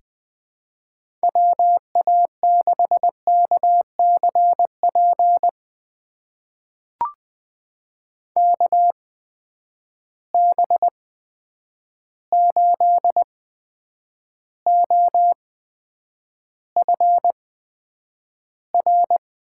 Morse code, heard as a keyed tone, about 20 wpm, spaced as for 9 wpm, 700 Hz. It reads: WA6KCP KB8OFR
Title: USA Call Signs - 6 Characters - 8x Character Spacing - 20wpm